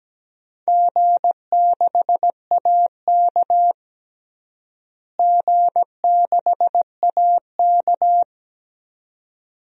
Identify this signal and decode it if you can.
Morse code, heard as a keyed tone, about 17 wpm, 700 Hz. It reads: G6AK G6AK
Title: DXCC-100 Call Signs - 4 Characters - Repeated 2x - 17wpm